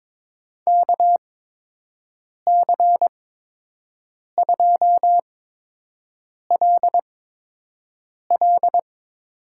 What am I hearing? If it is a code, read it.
Morse code, heard as a keyed tone, about 22 wpm, spaced as for 6 wpm, 700 Hz. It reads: KC2LL
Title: USA Call Signs - 8x Character Spacing - 22wpm